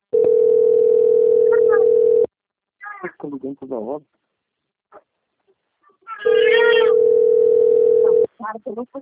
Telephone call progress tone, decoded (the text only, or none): ringback tone